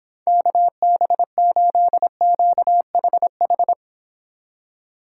Morse code, heard as a keyed tone, about 31 wpm, 700 Hz. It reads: KB8QHH